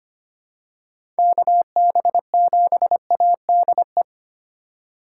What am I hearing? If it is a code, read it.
Morse code, heard as a keyed tone, about 25 wpm, 700 Hz. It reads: KB7ADE
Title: USA Call Signs - 2x Character Spacing - 25wpm